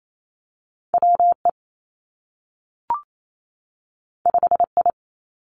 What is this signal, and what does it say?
Morse code, heard as a keyed tone, about 28 wpm, 700 Hz. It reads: W E 5 I
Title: Two Challenging Letter-Number Combinations 28wpm